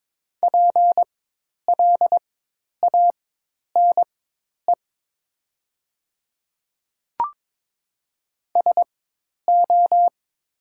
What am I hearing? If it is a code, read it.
Morse code, heard as a keyed tone, about 22 wpm, spaced as for 10 wpm, 700 Hz. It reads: PLANE SO
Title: 4x Character Spacing - Top 500 Words - 22wpm